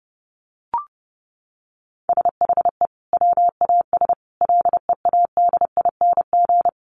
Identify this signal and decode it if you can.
Morse code, heard as a keyed tone, about 30 wpm, 700 Hz. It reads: SHE WAS LEADING
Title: Sentences from Top 1100 Words 30wpm